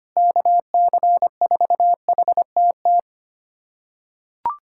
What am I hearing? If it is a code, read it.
Morse code, heard as a keyed tone, about 25 wpm, 700 Hz. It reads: KC4HTT
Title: USA Callsign DE Callsign 25wpm